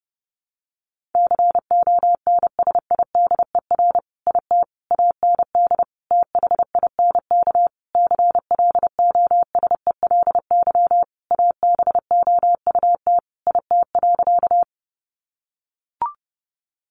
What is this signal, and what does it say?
Morse code, heard as a keyed tone, about 30 wpm, 700 Hz. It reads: CONSIDER IT AND THINK CLOSELY ABOUT IT.
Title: Sentences from Top 1400 Words 30wpm